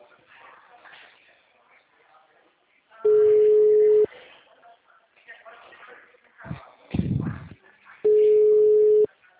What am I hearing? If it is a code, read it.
Ringback tone (Europe, 425 Hz), with speech.